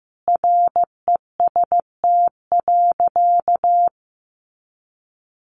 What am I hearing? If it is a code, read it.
Morse code, heard as a keyed tone, about 15 wpm, 700 Hz. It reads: REST.
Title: Sentences from Top 500 Words 15wpm